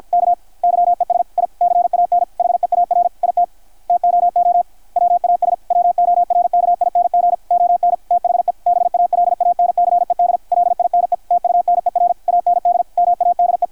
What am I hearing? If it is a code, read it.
Morse code, heard as a keyed tone, about 53 wpm, 660 Hz, with radio noise. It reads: K YES I CAN HEAR IT TOO WAS WORKING ON THE BALANCED LINE TUNER AND MADE
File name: archived / CRSnet